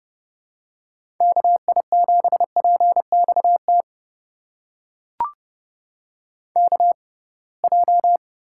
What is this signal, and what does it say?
Morse code, heard as a keyed tone, about 30 wpm, spaced as for 23 wpm, 700 Hz. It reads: KI7PXT KJ
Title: USA Call Signs - 6 Characters - 6x Character Spacing - 30wpm